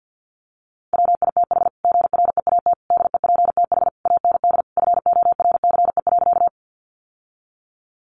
Morse code, heard as a keyed tone, about 50 wpm, 700 Hz. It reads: WITH GREAT DEPTH AND FORCE.
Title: Sentences from Top 1200 Words 50wpm